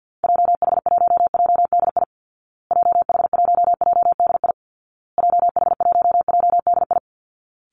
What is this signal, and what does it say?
Morse code, heard as a keyed tone, about 50 wpm, 700 Hz. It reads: JH1JDI JH1JDI JH1JDI
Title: Call Signs Repeated 2x - 50wpm